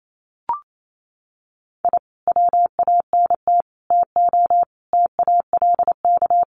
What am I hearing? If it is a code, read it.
Morse code, heard as a keyed tone, about 28 wpm, 700 Hz. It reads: I WANT TO TALK
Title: Sets of 5 Words Encore 28wpm